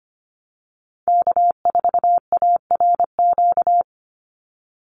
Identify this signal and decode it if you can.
Morse code, heard as a keyed tone, about 25 wpm, 700 Hz. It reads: K4ARQ
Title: Call Signs Repeated 3x - 25wpm